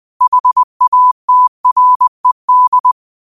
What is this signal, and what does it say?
Morse code, heard as a keyed tone, about 20 wpm, 1000 Hz. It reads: HATRED
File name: qotdfiles / 20wpm